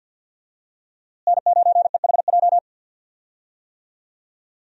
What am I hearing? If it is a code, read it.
Morse code, heard as a keyed tone, about 52 wpm, 700 Hz. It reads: N9EHJ
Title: USA Call Signs - 5 Characters - Repeated 2x - 50wpm